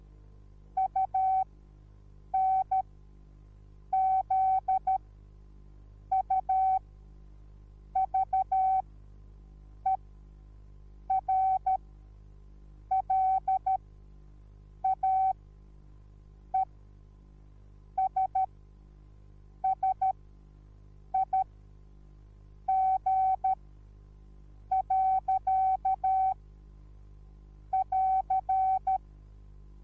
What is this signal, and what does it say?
Morse code, heard as a keyed tone, about 13 wpm, spaced as for 6 wpm, 760 Hz. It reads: UNZUVERLAESSIG.<AR>